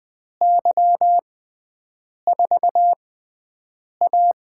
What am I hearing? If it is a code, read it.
Morse code, heard as a keyed tone, about 20 wpm, spaced as for 7 wpm, 700 Hz. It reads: Y4A
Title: USA Call Signs - 4 Characters - 6x Character Spacing - 20wpm